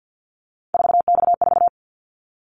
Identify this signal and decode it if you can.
Morse code, heard as a keyed tone, about 50 wpm, 700 Hz. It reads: 4X4
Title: Callsign Prefixes for Top 50 DXCC - 50wpm